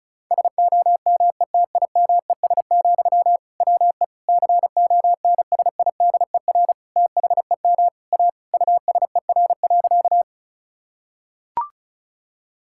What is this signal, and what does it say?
Morse code, heard as a keyed tone, about 34 wpm, 690 Hz. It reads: SOMETIMES, WE CONSIDER THEM A USER.
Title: Sentences from Top 1000 Words - Review Part 5 - 35wpm